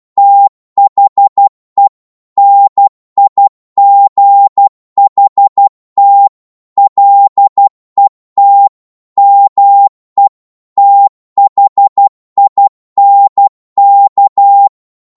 Morse code, heard as a keyed tone, about 12 wpm, 800 Hz. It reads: THE NIGHT LET ME THINK